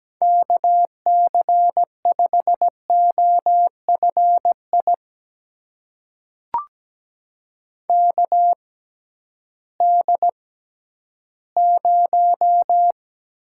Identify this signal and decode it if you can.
Morse code, heard as a keyed tone, about 17 wpm, spaced as for 11 wpm, 700 Hz. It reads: KC5OFI KD0
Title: USA Call Signs - 6 Characters - 6x Character Spacing - 17wpm